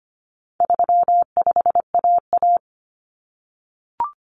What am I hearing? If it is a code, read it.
Morse code, heard as a keyed tone, about 25 wpm, 700 Hz. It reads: 35AA
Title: Common Cut Numbers - 4-Digit 25wpm